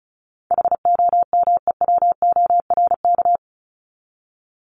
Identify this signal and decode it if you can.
Morse code, heard as a keyed tone, about 34 wpm, 700 Hz. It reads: HOMEWORK
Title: Top 1501-1750 Words 35wpm